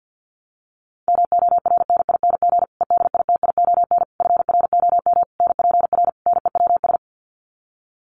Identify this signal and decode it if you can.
Morse code, heard as a keyed tone, about 50 wpm, 700 Hz. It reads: MORNING EDITION FROM NPR NEWS
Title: Sets of 5 Words Encore 50wpm